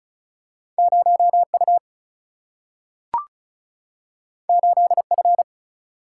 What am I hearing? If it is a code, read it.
Morse code, heard as a keyed tone, about 36 wpm, 700 Hz. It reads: 0U 8F
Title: Single-Letter Single-Number 35wpm